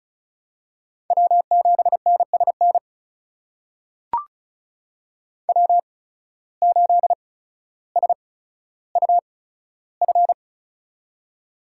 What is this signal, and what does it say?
Morse code, heard as a keyed tone, about 34 wpm, spaced as for 15 wpm, 700 Hz. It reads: W7NSN W8SUF
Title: USA Call Signs - 5 Characters - 8x Character Spacing - 35wpm